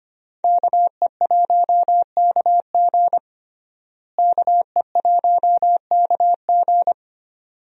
Morse code, heard as a keyed tone, about 25 wpm, 700 Hz. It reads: KE1KG KE1KG
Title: USA Call Signs - 5 Characters - Repeated 2x - 25wpm